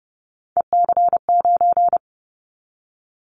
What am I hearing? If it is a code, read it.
Morse code, heard as a keyed tone, about 30 wpm, 700 Hz. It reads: EC9
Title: Callsign Prefixes for Top 100 DXCC - 30wpm